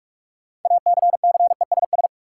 Morse code, heard as a keyed tone, about 41 wpm, 700 Hz. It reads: ACCESS
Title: Sentences from Top 1000 Words - Review Part 4 - New Word Drill - 45wpm